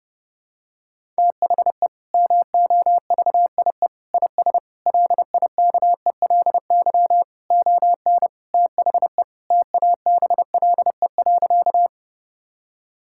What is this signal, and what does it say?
Morse code, heard as a keyed tone, about 32 wpm, 700 Hz. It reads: THE MOVIE IS LIKELY ON THE TABLE.
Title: Sentences from Top 1000 Words - Review Part 5 - 30wpm